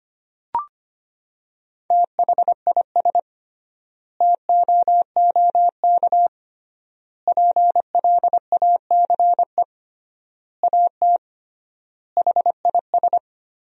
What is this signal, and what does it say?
Morse code, heard as a keyed tone, about 25 wpm, 700 Hz. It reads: THIS TOOK PLACE AT HIS
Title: 3x Word-Spacing - Sentences from Top 300 Words - 25wpm